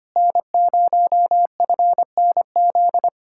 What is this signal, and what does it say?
Morse code, heard as a keyed tone, about 27 wpm, 700 Hz. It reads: N0FNZ
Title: Speed-Racing Common QSO Phrases Encore - 25wpm